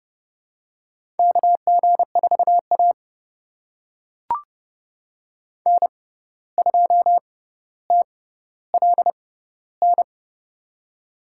Morse code, heard as a keyed tone, about 30 wpm, spaced as for 14 wpm, 700 Hz. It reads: KG4A N2TLN
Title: USA Call Signs - 6x Character Spacing - 30wpm